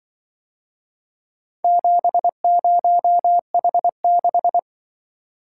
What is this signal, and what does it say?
Morse code, heard as a keyed tone, about 26 wpm, 700 Hz. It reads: 70H6